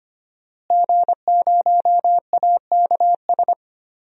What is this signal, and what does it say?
Morse code, heard as a keyed tone, about 25 wpm, 700 Hz. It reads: G0AKS
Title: DXCC-100 Call Signs - 5 Characters - Repeated 2x - 25wpm